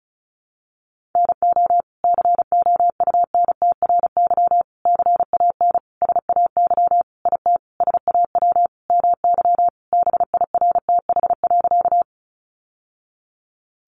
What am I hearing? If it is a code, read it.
Morse code, heard as a keyed tone, about 34 wpm, 700 Hz. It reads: NO COUNTRY CAN SAY IT SAW MY BIRTH.
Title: Sentences from Top 800 Words 35wpm